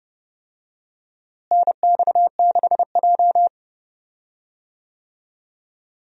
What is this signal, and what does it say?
Morse code, heard as a keyed tone, about 32 wpm, 700 Hz. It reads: NX6J
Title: USA Call Signs - 4 Characters - Repeated 3x - 30wpm